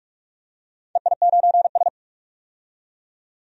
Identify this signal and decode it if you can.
Morse code, heard as a keyed tone, about 44 wpm, 700 Hz. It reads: EI9S
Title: DXCC-100 Call Signs - 4 Characters - 45wpm